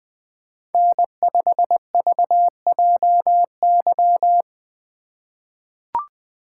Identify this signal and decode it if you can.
Morse code, heard as a keyed tone, about 20 wpm, 700 Hz. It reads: N5VJY
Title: USA Call Signs - 5 Characters - 2x Character Spacing - 20wpm